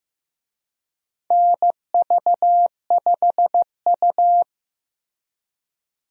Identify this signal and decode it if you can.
Morse code, heard as a keyed tone, about 15 wpm, 700 Hz. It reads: NV5U
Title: USA Call Signs - 4 Characters - Repeated 3x - 15wpm